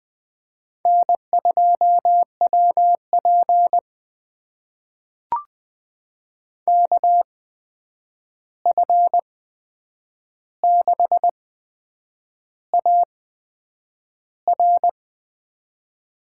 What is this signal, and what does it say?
Morse code, heard as a keyed tone, about 20 wpm, spaced as for 8 wpm, 700 Hz. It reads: N2WP KF6AR
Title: USA Call Signs - 8x Character Spacing - 20wpm